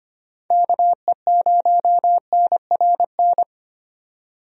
Morse code, heard as a keyed tone, about 25 wpm, 700 Hz. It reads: KE0NRN
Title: USA Call Signs - 6 Characters - 25wpm